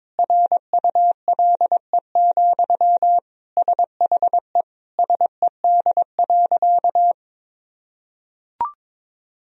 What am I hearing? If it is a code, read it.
Morse code, heard as a keyed tone, about 22 wpm, 700 Hz. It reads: RULE, SHE SED.
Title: Sentences from Top 400 Words and Abbreviations 22wpm